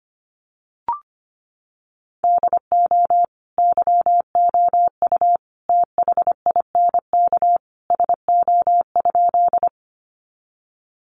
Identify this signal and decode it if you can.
Morse code, heard as a keyed tone, about 25 wpm, 700 Hz. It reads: DO YOU THINK SO?
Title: Sentences from Top 200 Words 25wpm